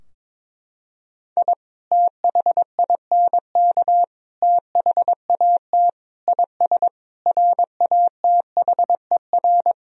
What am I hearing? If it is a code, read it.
Morse code, heard as a keyed tone, about 22 wpm, 700 Hz. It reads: I THINK THAT IS RATHER
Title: Sentences from Top 1000 Words - Review Part 4 - 22wpm